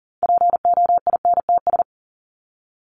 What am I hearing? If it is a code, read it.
Morse code, heard as a keyed tone, about 40 wpm, 700 Hz. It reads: POINTS
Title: ES:S Suffix - Top 100 Words with Base Words 40wpm